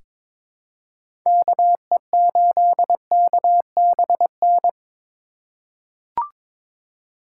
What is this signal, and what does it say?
Morse code, heard as a keyed tone, about 23 wpm, 700 Hz. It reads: KE8KBN
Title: USA Call Signs - 6 Characters - 3x Character Spacing - 22wpm